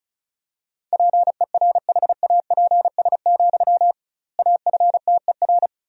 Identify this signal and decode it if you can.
Morse code, heard as a keyed tone, about 34 wpm, 700 Hz. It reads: PERHAPS, AFTER
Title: 2x Word-Spacing - Sentences from Top 500 Words - 35wpm